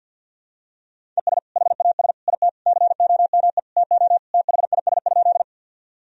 Morse code, heard as a keyed tone, about 49 wpm, 700 Hz. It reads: ES HAS IT COME TO THIS?
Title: Sentences from Top 100 Words and Abbreviations 50wpm